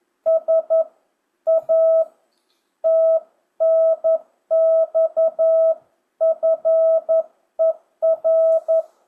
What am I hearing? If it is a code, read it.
Morse code, heard as a keyed tone, about 11 wpm, 660 Hz. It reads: S A TNX FER
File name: jh8cbh / 2025cw